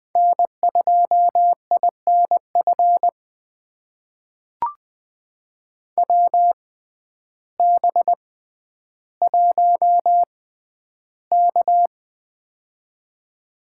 Morse code, heard as a keyed tone, about 20 wpm, spaced as for 11 wpm, 700 Hz. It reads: N2INF WB1K
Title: USA Call Signs - 6x Character Spacing - 20wpm